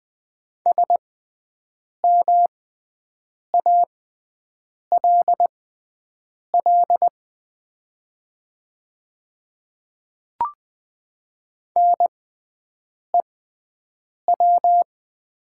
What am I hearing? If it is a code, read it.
Morse code, heard as a keyed tone, about 20 wpm, spaced as for 7 wpm, 700 Hz. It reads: SMALL NEW
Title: 6x Character Spacing - Top 300 Words - 20wpm